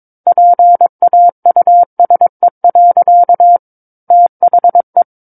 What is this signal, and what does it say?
Morse code, heard as a keyed tone, about 22 wpm, 700 Hz. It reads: PAUSE. THE
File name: 06-Chapter-06 22 WPM